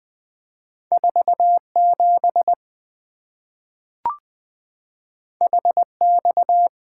Morse code, heard as a keyed tone, about 20 wpm, 690 Hz. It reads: 47 HX
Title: Two Letter-Number Combinations - 20wpm